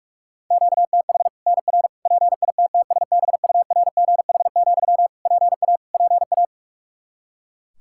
Morse code, heard as a keyed tone, about 46 wpm, 700 Hz. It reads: QTH NR PITTSBURGH, PA PA
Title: Practice QSO - Exchange 1 - Beginning - 45wpm